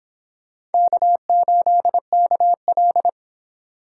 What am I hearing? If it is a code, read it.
Morse code, heard as a keyed tone, about 28 wpm, 700 Hz. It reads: K8KL